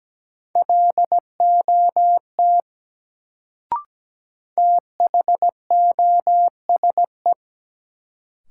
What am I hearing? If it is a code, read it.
Morse code, heard as a keyed tone, about 17 wpm, 700 Hz. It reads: LOT THOSE
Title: Sets of 2 Words - New Word Drill - 17wpm